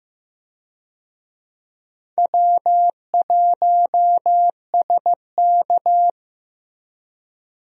Morse code, heard as a keyed tone, about 15 wpm, 700 Hz. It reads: W1SK
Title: USA Call Signs - 4 Characters - Repeated 3x - 15wpm